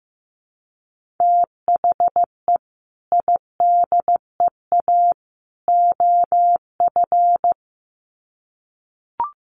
Morse code, heard as a keyed tone, about 15 wpm, 700 Hz. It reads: THE IDEA OF
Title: Sets of 3 Words 15wpm